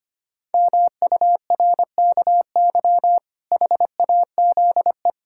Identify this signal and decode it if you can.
Morse code, heard as a keyed tone, about 25 wpm, 700 Hz. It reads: MURKY HAZE
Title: Pangram Challenge v2 - 25wpm